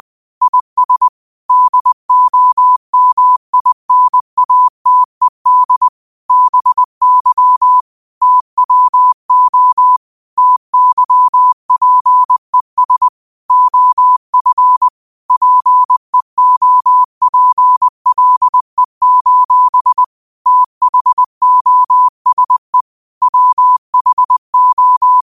Morse code, heard as a keyed tone, about 20 wpm, 1000 Hz. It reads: IS DOMINATED BY TWO TYPES OF PEOPLE: THOSE WHO